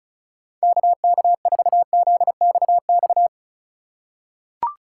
Morse code, heard as a keyed tone, about 34 wpm, 690 Hz. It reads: KK4ZXX
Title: USA Call Signs - 4x Character Spacing - 35wpm